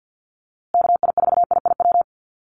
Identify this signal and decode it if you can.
Morse code, heard as a keyed tone, about 50 wpm, 700 Hz. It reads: KI4IIW
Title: USA Call Signs Repeated 3x 50wpm